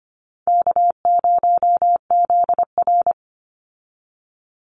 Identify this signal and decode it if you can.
Morse code, heard as a keyed tone, about 25 wpm, 700 Hz. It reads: K0ZR
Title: Easy Call Signs - 25wpm